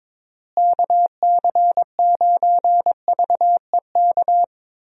Morse code, heard as a keyed tone, about 21 wpm, 700 Hz. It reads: KC9VEK
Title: USA Call Signs - 6 Characters - Repeated 3x - 22wpm